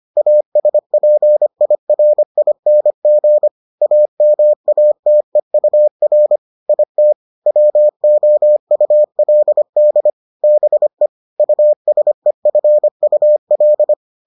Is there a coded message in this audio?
Yes: Morse code from 0.2 to 13.9 s, heard as a keyed tone, about 25 wpm, 600 Hz. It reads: ASPIRING AMATEUR IT WOULD BE USEFUL